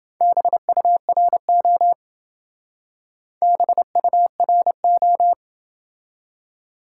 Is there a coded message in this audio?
Yes: Morse code from 0.2 to 5.3 s, heard as a keyed tone, about 27 wpm, 700 Hz. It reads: BURO BURO